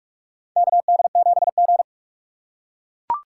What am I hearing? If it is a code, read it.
Morse code, heard as a keyed tone, about 44 wpm, 700 Hz. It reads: KD7G
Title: USA Call Signs - 3x Character Spacing - 45wpm